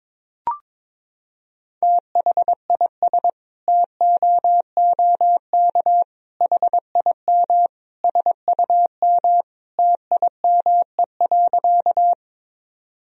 Morse code, heard as a keyed tone, about 22 wpm, 700 Hz. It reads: THIS TOOK HIM SUM TIME.
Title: Sentences from Top 300 Words and Abbreviations 22wpm